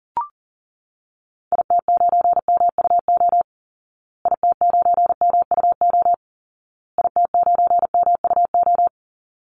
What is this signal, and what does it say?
Morse code, heard as a keyed tone, about 40 wpm, 700 Hz. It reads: IT9MUO IT9MUO IT9MUO
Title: Call Signs Repeated 2x - 40wpm